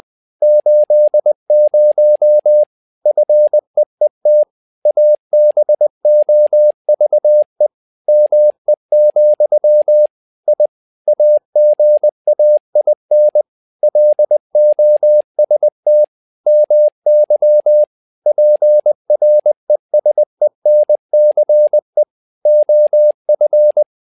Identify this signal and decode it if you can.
Morse code, heard as a keyed tone, about 20 wpm, 600 Hz. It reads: 80 FEET ABOVE ME, I AGAIN LOST MY PRESENCE OF
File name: learning center / tower trouble / tower trouble 20wpm